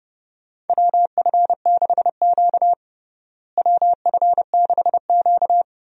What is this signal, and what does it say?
Morse code, heard as a keyed tone, about 32 wpm, 700 Hz. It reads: WF6Q WF6Q
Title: USA Call Signs - 4 Characters - Repeated 3x - 30wpm